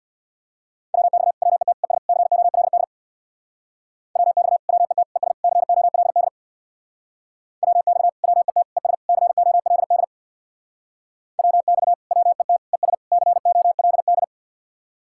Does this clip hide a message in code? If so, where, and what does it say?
Yes: Morse code from 0.9 to 14.2 s, heard as a keyed tone, about 63 wpm, 700 Hz. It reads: WX WET ES COLD WX WET ES COLD WX WET ES COLD WX WET ES COLD